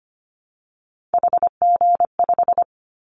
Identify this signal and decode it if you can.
Morse code, heard as a keyed tone, about 25 wpm, 700 Hz. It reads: HG5
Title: Callsign Prefixes for Top 25 DXCC - Repeated 3x - 25wpm